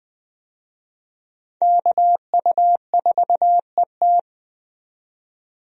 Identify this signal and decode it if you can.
Morse code, heard as a keyed tone, about 20 wpm, 700 Hz. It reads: KU4ET
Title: USA Call Signs - 5 Characters - 20wpm